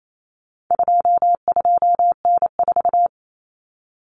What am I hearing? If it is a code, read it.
Morse code, heard as a keyed tone, about 28 wpm, 700 Hz. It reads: 22N4
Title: Common Cut Numbers - 4-Digit 28wpm